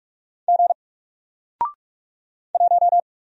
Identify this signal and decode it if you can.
Morse code, heard as a keyed tone, about 46 wpm, 700 Hz. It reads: G 1
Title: Single Letter-Number 45wpm